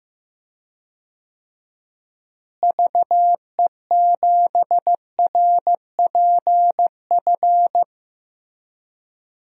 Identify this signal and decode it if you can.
Morse code, heard as a keyed tone, about 15 wpm, 700 Hz. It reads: VE7RPF
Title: DXCC-100 Call Signs - 6 Characters - Repeated 2x - 15wpm